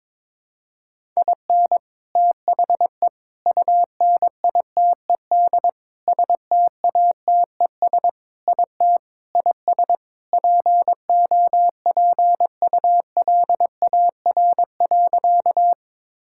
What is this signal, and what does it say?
Morse code, heard as a keyed tone, about 22 wpm, 700 Hz. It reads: IN THE UNITED STATES IT IS POPULAR.
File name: practice / 22wpm / Sentences with 3-Word Phrases - 22wpm